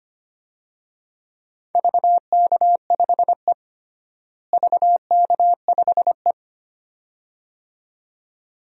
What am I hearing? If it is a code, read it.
Morse code, heard as a keyed tone, about 25 wpm, 700 Hz. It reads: VK5E VK5E
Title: DXCC-100 Call Signs - 4 Characters - Repeated 2x - 25wpm